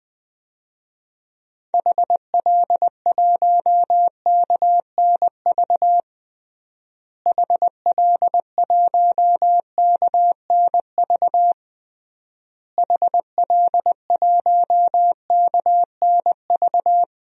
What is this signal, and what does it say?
Morse code, heard as a keyed tone, about 20 wpm, 700 Hz. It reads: HL1KNV HL1KNV HL1KNV
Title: DXCC-100 Call Signs - 6 Characters - Repeated 3x - 20wpm